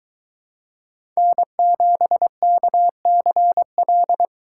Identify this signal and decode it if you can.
Morse code, heard as a keyed tone, about 23 wpm, 700 Hz. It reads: N7KCL